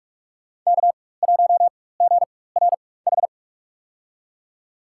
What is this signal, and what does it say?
Morse code, heard as a keyed tone, about 45 wpm, spaced as for 21 wpm, 700 Hz. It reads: K1GRH
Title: USA Call Signs - 5 Characters - 4x Character Spacing - 45wpm